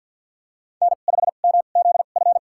Morse code, heard as a keyed tone, about 52 wpm, spaced as for 36 wpm, 700 Hz. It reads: N5MZF
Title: USA Call Signs - 5 Characters - 2x Character Spacing - 50wpm